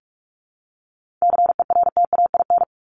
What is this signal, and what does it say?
Morse code, heard as a keyed tone, about 44 wpm, 700 Hz. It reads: CERTAIN
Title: LY Suffix - Top 100 Words with Base Words 45wpm